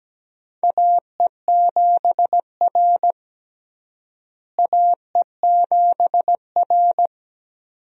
Morse code, heard as a keyed tone, about 17 wpm, 700 Hz. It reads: AE7R AE7R
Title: USA Call Signs - 4 Characters - Repeated 2x - 17wpm